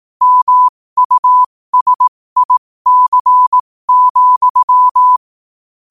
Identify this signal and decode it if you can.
Morse code, heard as a keyed tone, about 18 wpm, spaced as for 15 wpm, 1000 Hz. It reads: MUSIC,
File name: qotdfiles / 15wpm